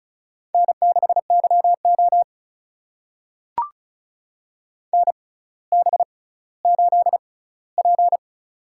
Morse code, heard as a keyed tone, about 36 wpm, spaced as for 18 wpm, 700 Hz. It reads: N6YO NB8P
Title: USA Call Signs - 4 Characters - 6x Character Spacing - 35wpm